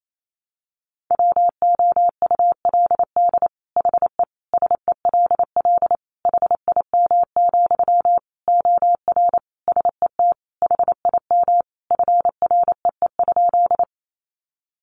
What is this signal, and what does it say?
Morse code, heard as a keyed tone, about 28 wpm, 700 Hz. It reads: WOULD HE SELL HIM, OR SET HIM FREE?
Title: Sentences from Top 1100 Words 28wpm